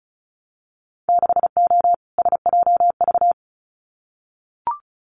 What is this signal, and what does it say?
Morse code, heard as a keyed tone, about 36 wpm, 700 Hz. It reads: 6O SJV
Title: Winter Field Day Exchanges Encore 35wpm